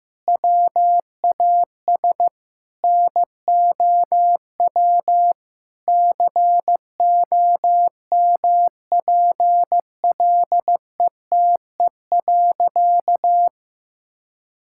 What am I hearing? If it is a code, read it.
Morse code, heard as a keyed tone, about 15 wpm, 700 Hz. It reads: WAS NOW COMPLETE.
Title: Sentences from Top 1000 Words - Review Part 6 - 15wpm